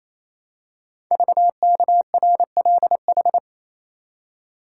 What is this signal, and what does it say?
Morse code, heard as a keyed tone, about 28 wpm, 700 Hz. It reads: VKRLH
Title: Code Groups - 5 Letters - 28wpm